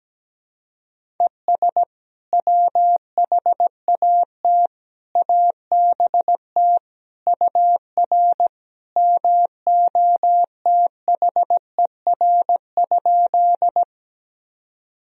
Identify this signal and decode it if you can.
Morse code, heard as a keyed tone, about 17 wpm, 700 Hz. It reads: ES WHAT ABT UR MOTHER?
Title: Sentences from Top 200 Words and Abbreviations 17wpm